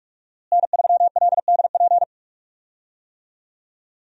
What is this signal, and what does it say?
Morse code, heard as a keyed tone, about 43 wpm, 700 Hz. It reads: N3LDP
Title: USA Call Signs - 5 Characters - Repeated 3x - 45wpm